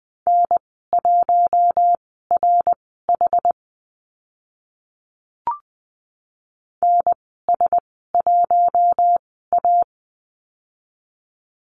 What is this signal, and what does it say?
Morse code, heard as a keyed tone, about 20 wpm, spaced as for 14 wpm, 700 Hz. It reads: N1RH NS1A
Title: 076 Lesson 13g - Letter L - Call Signs 20wpm